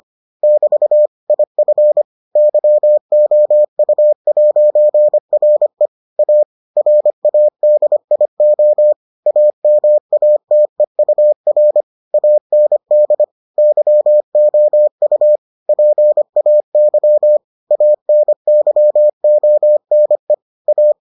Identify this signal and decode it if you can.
Morse code, heard as a keyed tone, about 25 wpm, 600 Hz. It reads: <BT> IF YOU'RE A RADIO AMATEUR AND YOU PAY ANYONE A